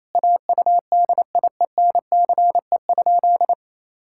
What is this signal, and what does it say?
Morse code, heard as a keyed tone, about 28 wpm, 700 Hz. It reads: AUDIENCE?
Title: Sentences from Top 1000 Words - Review Part 5 - 28wpm